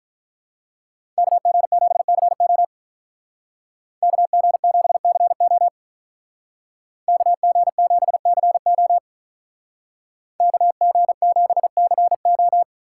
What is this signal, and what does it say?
Morse code, heard as a keyed tone, about 46 wpm, 700 Hz. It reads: KG7CO KG7CO KG7CO KG7CO